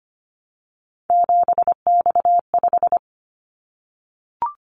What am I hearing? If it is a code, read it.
Morse code, heard as a keyed tone, about 25 wpm, 700 Hz. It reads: 7X5
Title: Callsign Prefixes for Top 100 DXCC - 25wpm